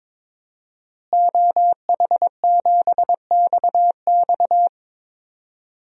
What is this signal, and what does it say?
Morse code, heard as a keyed tone, about 22 wpm, 700 Hz. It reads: OH7XX
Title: DXCC-100 Call Signs - 5 Characters - Repeated 2x - 22wpm